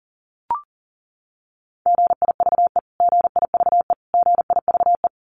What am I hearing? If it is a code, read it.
Morse code, heard as a keyed tone, about 40 wpm, 700 Hz. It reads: GIVE GIVE GIVE
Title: Top 200 Words Repeated 40wpm